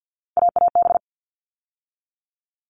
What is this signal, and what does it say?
Morse code, heard as a keyed tone, about 48 wpm, 700 Hz. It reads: AAB
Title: All Cut Numbers - 3-Digit 50wpm